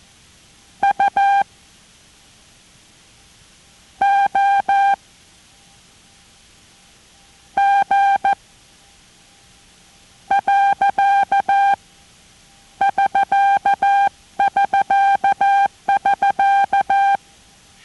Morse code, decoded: U O G . <SK><SK><SK>